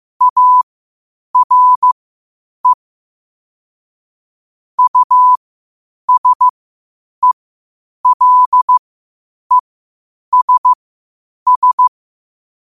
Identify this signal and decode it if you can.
Morse code, heard as a keyed tone, about 15 wpm, spaced as for 8 wpm, 1000 Hz. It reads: ARE USELESS
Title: 8 WPM morse code quotes for Thu, 29 Feb 2024